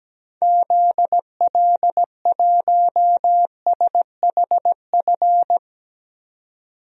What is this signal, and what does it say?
Morse code, heard as a keyed tone, about 17 wpm, 700 Hz. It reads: ZL1SHF
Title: DXCC-100 Call Signs - 6 Characters - 17wpm